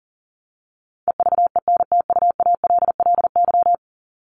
Morse code, heard as a keyed tone, about 40 wpm, 700 Hz. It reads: EVENTUALLY
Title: Top 751-1000 Words 40wpm